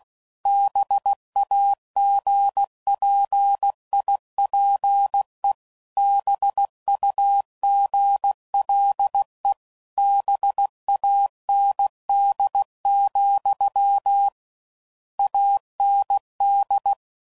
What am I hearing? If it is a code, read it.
Morse code, heard as a keyed tone, about 16 wpm, 800 Hz. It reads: BAGPIPE BUGLE BAND, AND